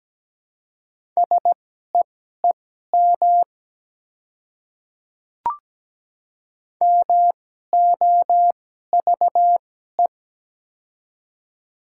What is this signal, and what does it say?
Morse code, heard as a keyed tone, about 17 wpm, spaced as for 12 wpm, 700 Hz. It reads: SEEM MOVE
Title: 2x Character Spacing - Top 300 Words - 17wpm